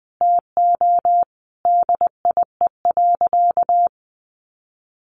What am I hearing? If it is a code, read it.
Morse code, heard as a keyed tone, about 20 wpm, 700 Hz. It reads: TO DIE.
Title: Sentences from Top 1300 Words 20wpm